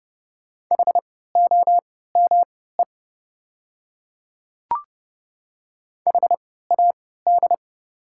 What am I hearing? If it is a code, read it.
Morse code, heard as a keyed tone, about 30 wpm, spaced as for 17 wpm, 700 Hz. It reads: HOME HAD
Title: 3x Character Spacing - Top 500 Words - 30wpm